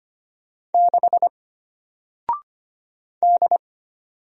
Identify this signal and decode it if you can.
Morse code, heard as a keyed tone, about 25 wpm, 700 Hz. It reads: 6 D
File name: Single-Letter-Number-25wpm